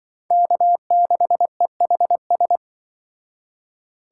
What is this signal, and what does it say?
Morse code, heard as a keyed tone, about 24 wpm, 700 Hz. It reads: K6EHS
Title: Speed Racing - USA 1x3 Call Signs 20wpm